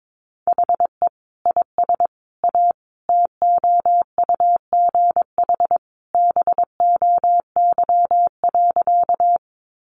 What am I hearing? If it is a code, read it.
Morse code, heard as a keyed tone, about 22 wpm, 700 Hz. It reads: HE IS A TOUGH BOY.
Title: Sentences from Top 1200 Words 22wpm